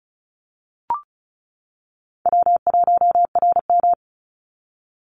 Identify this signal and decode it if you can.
Morse code, heard as a keyed tone, about 34 wpm, 700 Hz. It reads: W1RM
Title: Call Signs - 35wpm